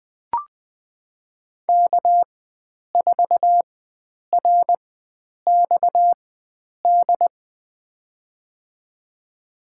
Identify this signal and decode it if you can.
Morse code, heard as a keyed tone, about 20 wpm, spaced as for 9 wpm, 700 Hz. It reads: K4RXD
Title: USA Call Signs - 5 Characters - 4x Character Spacing - 20wpm